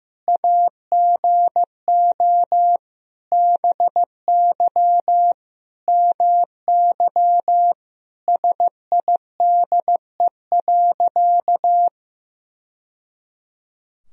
Morse code, heard as a keyed tone, about 16 wpm, 700 Hz. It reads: AGO BY MY SIDE.
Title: Sentences from Top 1000 Words - Review Part 3 - 15wpm